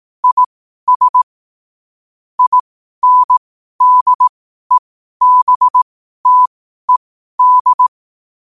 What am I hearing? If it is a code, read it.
Morse code, heard as a keyed tone, about 18 wpm, spaced as for 12 wpm, 1000 Hz. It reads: IS INDEBTED
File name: qotdfiles / 12wpm